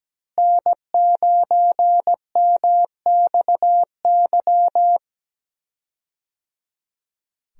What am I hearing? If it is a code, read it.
Morse code, heard as a keyed tone, about 17 wpm, 700 Hz. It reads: N9MXY
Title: USA Call Signs - 5 Characters - Repeated 3x - 17wpm